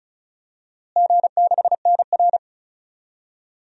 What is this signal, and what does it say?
Morse code, heard as a keyed tone, about 36 wpm, 700 Hz. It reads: G6NR
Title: DXCC-100 Call Signs - 4 Characters - Repeated 3x - 35wpm